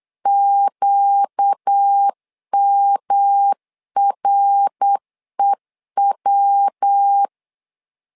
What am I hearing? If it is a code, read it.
Morse code, heard as a keyed tone, about 8 wpm, 800 Hz. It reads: QMREW
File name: LEZIONI / 4 Q 8wpm 2732